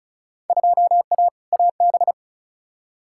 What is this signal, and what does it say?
Morse code, heard as a keyed tone, about 34 wpm, 700 Hz. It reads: 2A AB
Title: Field Day Exchanges - 35wpm